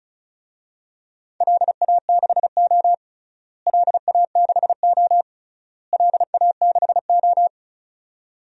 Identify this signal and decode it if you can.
Morse code, heard as a keyed tone, about 36 wpm, 700 Hz. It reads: LA6O LA6O LA6O
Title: DXCC-100 Call Signs - 4 Characters - Repeated 3x - 35wpm